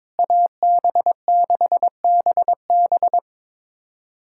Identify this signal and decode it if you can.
Morse code, heard as a keyed tone, about 22 wpm, 700 Hz. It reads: AB6BB
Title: USA Callsign DE Callsign - 5 Characters - 22wpm